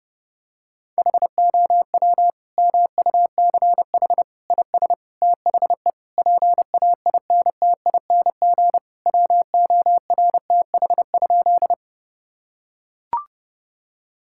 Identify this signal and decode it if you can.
Morse code, heard as a keyed tone, about 28 wpm, 700 Hz. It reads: HOW MUCH IS THE PAINTING WORTH?
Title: Sentences from Top 1000 Words - Review Part 6 - 30wpm